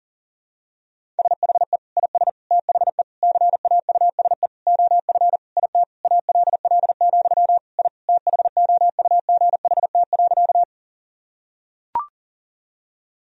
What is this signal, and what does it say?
Morse code, heard as a keyed tone, about 40 wpm, 700 Hz. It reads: SHE IS THE CAUSE OF IT ALL, I THOUGHT.
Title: Sentences from Top 1000 Words - Review Part 1 - 40wpm